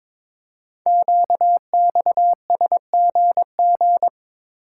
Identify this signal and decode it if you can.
Morse code, heard as a keyed tone, about 22 wpm, 700 Hz. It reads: QXSGG
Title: Code Groups - 5 Letter:Numbers - 22wpm